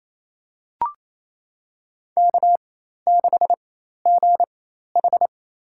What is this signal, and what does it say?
Morse code, heard as a keyed tone, about 28 wpm, spaced as for 13 wpm, 700 Hz. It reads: K6GH
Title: USA Call Signs - 4 Characters - 4x Character Spacing - 28wpm